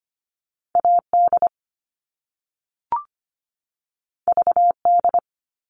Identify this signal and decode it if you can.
Morse code, heard as a keyed tone, about 25 wpm, 700 Hz. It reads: AD VD
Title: All Cut Numbers - 2-Digit 25wpm